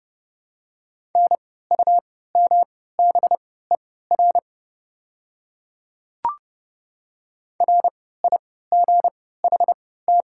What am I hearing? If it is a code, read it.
Morse code, heard as a keyed tone, about 29 wpm, spaced as for 17 wpm, 700 Hz. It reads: NUMBER RIGHT
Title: 3x Character Spacing - Top 500 Words - 30wpm